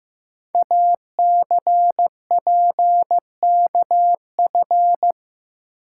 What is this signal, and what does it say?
Morse code, heard as a keyed tone, about 15 wpm, 700 Hz. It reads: ACPKF
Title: Code Groups - 5 Letter:Numbers - 15wpm